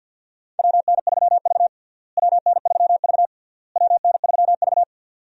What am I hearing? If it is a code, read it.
Morse code, heard as a keyed tone, about 51 wpm, 700 Hz. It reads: WN3V WN3V WN3V
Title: USA Call Signs - 4 Characters - Repeated 3x - 50wpm